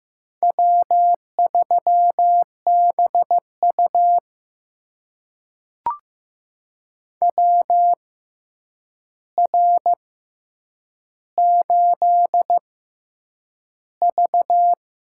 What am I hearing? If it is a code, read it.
Morse code, heard as a keyed tone, about 15 wpm, spaced as for 8 wpm, 700 Hz. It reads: W3BU WR8V
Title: USA Call Signs - 4 Characters - 6x Character Spacing - 15wpm